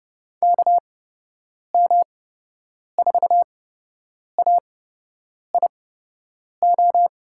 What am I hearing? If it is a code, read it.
Morse code, heard as a keyed tone, about 31 wpm, spaced as for 8 wpm, 700 Hz. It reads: KM4AIO
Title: USA Call Signs - 6 Characters - 8x Character Spacing - 30wpm